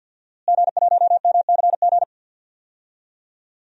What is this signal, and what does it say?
Morse code, heard as a keyed tone, about 52 wpm, 700 Hz. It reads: G1MCG
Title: DXCC-100 Call Signs - 5 Characters - Repeated 3x - 50wpm